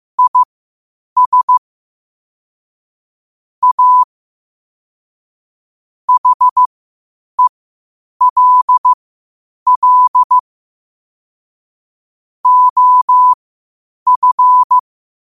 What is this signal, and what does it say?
Morse code, heard as a keyed tone, about 15 wpm, spaced as for 8 wpm, 1000 Hz. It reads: IS A HELL OF
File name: qotdfiles / 8wpm